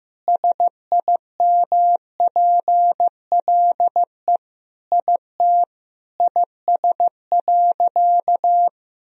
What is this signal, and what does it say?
Morse code, heard as a keyed tone, about 15 wpm, 700 Hz. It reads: SIMPLE IT IS.
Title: Sentences from Top 400 Words and Abbreviations 15wpm